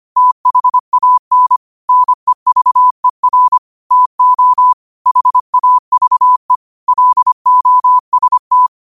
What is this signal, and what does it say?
Morse code, heard as a keyed tone, about 25 wpm, 1000 Hz. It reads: THAN NEVER TO HAVE LOST
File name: qotdfiles / 25wpm